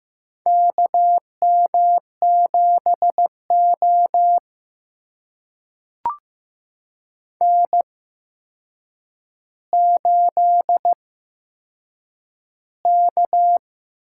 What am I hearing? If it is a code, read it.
Morse code, heard as a keyed tone, about 15 wpm, spaced as for 7 wpm, 700 Hz. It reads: KM7O N8K
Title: USA Call Signs - 4 Characters - 8x Character Spacing - 15wpm